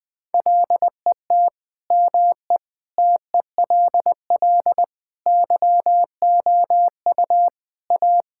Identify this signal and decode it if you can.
Morse code, heard as a keyed tone, about 20 wpm, 700 Hz. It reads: LET ME TELL YOU A
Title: Sentences with 4-Word Phrases - 20wpm